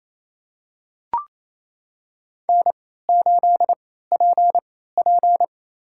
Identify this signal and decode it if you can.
Morse code, heard as a keyed tone, about 28 wpm, spaced as for 16 wpm, 700 Hz. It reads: N8PP
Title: USA Call Signs - 4 Characters - 3x Character Spacing - 28wpm